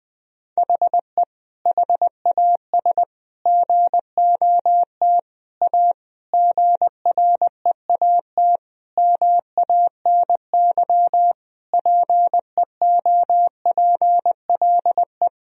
Morse code, heard as a keyed tone, about 20 wpm, 700 Hz. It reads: HE HAS GOT A GREAT MANY PEOPLE
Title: Sentences from Top 1000 Words - Review Part 1 - 20wpm